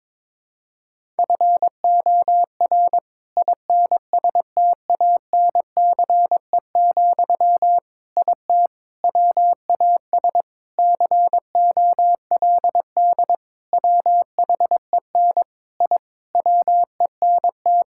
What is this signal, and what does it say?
Morse code, heard as a keyed tone, about 22 wpm, 700 Hz. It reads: FOR INSTANCE, IT WAS COLD WHEN I WENT
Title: Sentences from Top 1000 Words - Review Part 5 - 22wpm